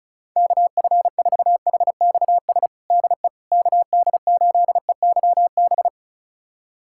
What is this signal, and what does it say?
Morse code, heard as a keyed tone, about 36 wpm, 700 Hz. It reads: KF4HXS DE KD8EYB
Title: USA Callsign DE Callsign - 6 Characters - 35wpm